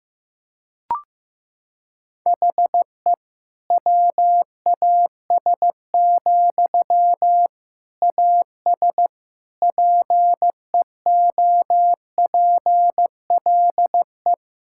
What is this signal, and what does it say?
Morse code, heard as a keyed tone, about 16 wpm, 700 Hz. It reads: HE WAS, AS PEOPLE
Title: Sentences from Top 1000 Words - Review Part 6 - 15wpm